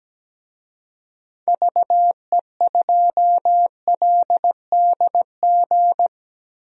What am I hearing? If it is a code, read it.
Morse code, heard as a keyed tone, about 17 wpm, 700 Hz. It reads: VE2LDG
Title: DXCC-100 Call Signs - 6 Characters - 17wpm